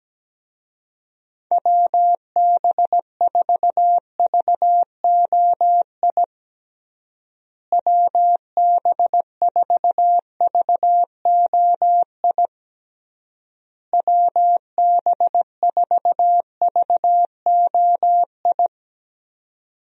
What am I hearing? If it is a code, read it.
Morse code, heard as a keyed tone, about 17 wpm, 700 Hz. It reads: WB4VOI WB4VOI WB4VOI
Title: USA Call Signs - 6 Characters - Repeated 3x - 17wpm